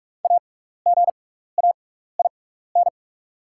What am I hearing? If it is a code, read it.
Morse code, heard as a keyed tone, about 44 wpm, spaced as for 15 wpm, 700 Hz. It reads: AGAIN
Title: 6x Character Spacing - Top 500 Words - 45wpm